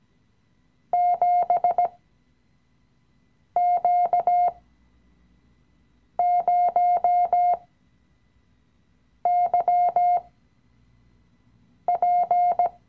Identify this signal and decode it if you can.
Morse code, heard as a keyed tone, about 17 wpm, spaced as for 5 wpm, 710 Hz. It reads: 7Q0YP
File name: CW-Online-Kurs / Anfänger / Lektion 018